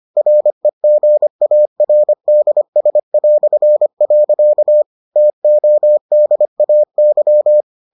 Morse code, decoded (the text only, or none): REGARDS". TODAY